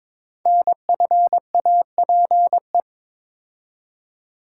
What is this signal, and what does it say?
Morse code, heard as a keyed tone, about 23 wpm, 700 Hz. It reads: NFAPE
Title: Code Groups - 5 Letters - 22wpm